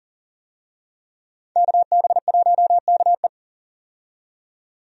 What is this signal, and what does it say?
Morse code, heard as a keyed tone, about 39 wpm, 700 Hz. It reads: KB1KE
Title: USA Call Signs - 5 Characters - 40wpm